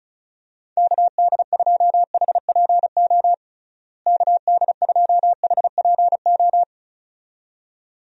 Morse code, heard as a keyed tone, about 36 wpm, 700 Hz. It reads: KD2HPO KD2HPO
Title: USA Call Signs - 6 Characters - Repeated 2x - 35wpm